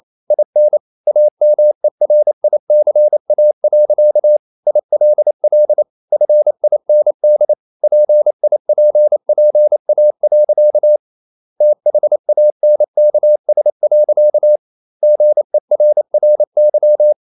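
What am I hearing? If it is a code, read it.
Morse code, heard as a keyed tone, about 28 wpm, 600 Hz. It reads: IN AMERICA. ILL FIND PIPPA. THANKS. GERRY
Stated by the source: Morse code practice at 28 wpm, text of Ken Follett's novel Never in English, from CW med Gnister.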